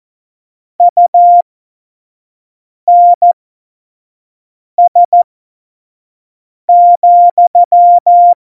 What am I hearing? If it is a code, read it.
Morse code, heard as a keyed tone, about 14 wpm, spaced as for 5 wpm, 700 Hz. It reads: UNS,
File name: Morsekurs / Witze / Witz 08. Dezember2020, 25 F70